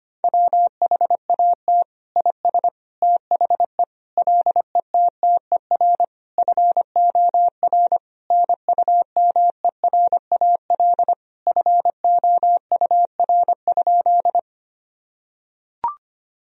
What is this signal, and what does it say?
Morse code, heard as a keyed tone, about 25 wpm, 700 Hz. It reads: WHAT IS THE LETTER FOR NUMERAL FOUR?
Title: Sentences from Top 1000 Words - Review Part 2 - 25wpm